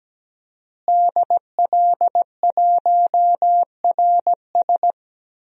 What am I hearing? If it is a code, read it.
Morse code, heard as a keyed tone, about 17 wpm, 700 Hz. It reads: DL1RS
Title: DXCC-100 Call Signs - 5 Characters - Repeated 3x - 17wpm